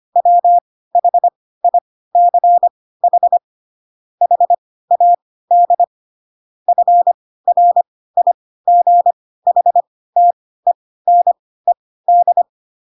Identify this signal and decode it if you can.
Morse code, heard as a keyed tone, about 25 wpm, spaced as for 16 wpm, 700 Hz. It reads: WHICH HAD FRIGHTENED
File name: Pinocchio / Pinocchio ch2 10 25 ik1hge